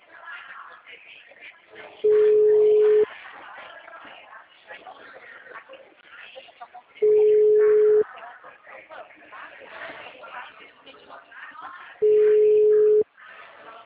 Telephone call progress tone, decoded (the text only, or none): ringback tone